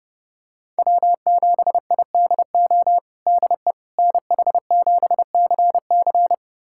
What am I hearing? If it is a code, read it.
Morse code, heard as a keyed tone, about 30 wpm, 700 Hz. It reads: W7IDO DE NH7CC
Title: USA Callsign DE Callsign - 5 Characters - 30wpm